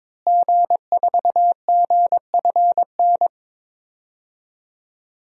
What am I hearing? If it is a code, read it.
Morse code, heard as a keyed tone, about 22 wpm, 700 Hz. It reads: G4GFN
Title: DXCC-100 Call Signs - 5 Characters - Repeated 3x - 22wpm